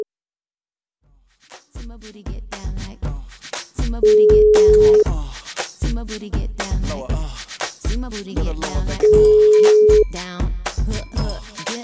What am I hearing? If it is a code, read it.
Ringback tone (Europe, 425 Hz).